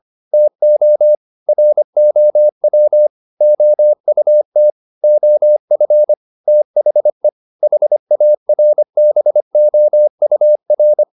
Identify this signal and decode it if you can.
Morse code, heard as a keyed tone, about 25 wpm, 600 Hz. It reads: TO ROW OUT OF THE HARBOUR
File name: the old man and the sea part 1 25wpm